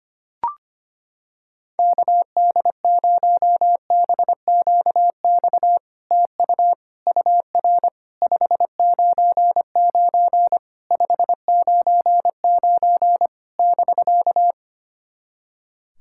Morse code, heard as a keyed tone, about 25 wpm, 700 Hz. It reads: KD0BQX TU UR 599 599 <BK>
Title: SOTA QSO Practice 25wpm